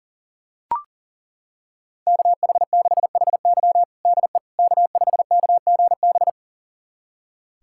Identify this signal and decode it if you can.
Morse code, heard as a keyed tone, about 40 wpm, 700 Hz. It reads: KH6HY DE K5KGB
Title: USA Callsign DE Callsign - 5 Characters - 40wpm